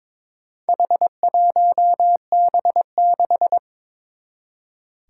Morse code, heard as a keyed tone, about 22 wpm, 700 Hz. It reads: H1B6
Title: Four Challenging Letter-Number Combinations 22wpm